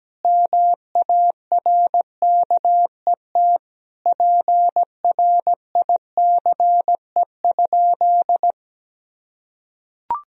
Morse code, heard as a keyed tone, about 17 wpm, 700 Hz. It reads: MARKET PRICE?
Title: Sentences from Top 1000 Words - Review Part 3 - 17wpm